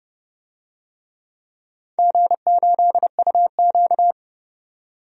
Morse code, heard as a keyed tone, about 30 wpm, 700 Hz. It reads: G8UQ
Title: DXCC-100 Call Signs - 4 Characters - 30wpm